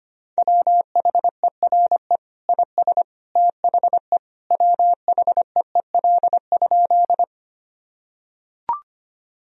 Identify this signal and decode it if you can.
Morse code, heard as a keyed tone, about 25 wpm, 700 Hz. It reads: WHERE IS THE WHEEL?
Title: Sentences from Top 1000 Words - Review Part 3 - 25wpm